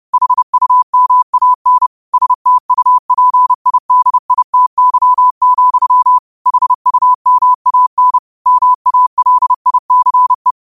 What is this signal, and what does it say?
Morse code, heard as a keyed tone, about 30 wpm, 1000 Hz. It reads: HUMAN STUPIDITY, HUMAN MALICE
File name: qotdfiles / 30wpm